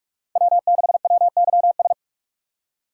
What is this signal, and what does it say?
Morse code, heard as a keyed tone, about 44 wpm, 690 Hz. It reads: W6WYS
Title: USA Call Signs - 5 Characters - 6x Character Spacing - 45wpm